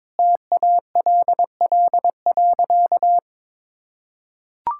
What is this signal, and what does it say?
Morse code, heard as a keyed tone, about 22 wpm, 700 Hz. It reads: TALL.
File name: practice / 22wpm / Easy Sentences with Hints - 22wpm